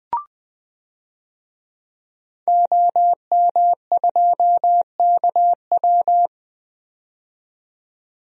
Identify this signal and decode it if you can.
Morse code, heard as a keyed tone, about 20 wpm, 700 Hz. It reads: OM2KW
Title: DXCC-100 Call Signs - 5 Characters - 20wpm